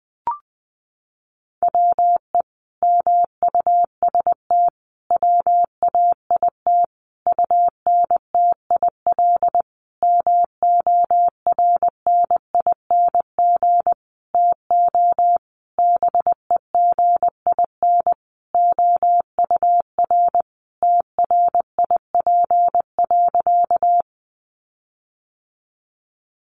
Morse code, heard as a keyed tone, about 20 wpm, 700 Hz. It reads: WE MUST WAIT UNTIL MORNING TO BEGIN OUR TRIP.
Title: Sentences from Top 1300 Words 20wpm